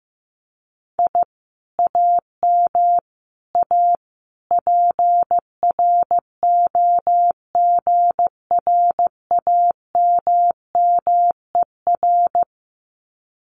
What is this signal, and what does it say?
Morse code, heard as a keyed tone, about 15 wpm, 700 Hz. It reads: I AM A PROGRAMMER
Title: Common QSO Phrases Encore 15wpm